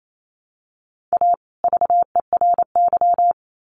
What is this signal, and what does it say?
Morse code, heard as a keyed tone, about 28 wpm, 700 Hz. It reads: A VERY
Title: Sets of 2 Words 28wpm